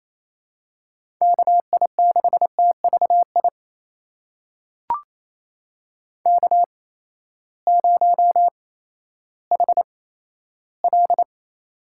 Morse code, heard as a keyed tone, about 28 wpm, spaced as for 14 wpm, 700 Hz. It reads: KI6TVI K0HL
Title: USA Call Signs - 8x Character Spacing - 28wpm